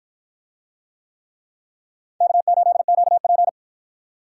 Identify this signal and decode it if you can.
Morse code, heard as a keyed tone, about 48 wpm, 700 Hz. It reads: K8QP